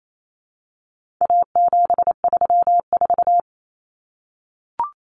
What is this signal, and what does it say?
Morse code, heard as a keyed tone, about 28 wpm, 700 Hz. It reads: A734
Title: Common Cut Numbers - 4-Digit 28wpm